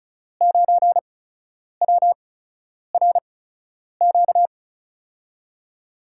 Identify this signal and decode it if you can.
Morse code, heard as a keyed tone, about 36 wpm, spaced as for 10 wpm, 700 Hz. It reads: 9WRQ
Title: USA Call Signs - 5 Characters - 8x Character Spacing - 35wpm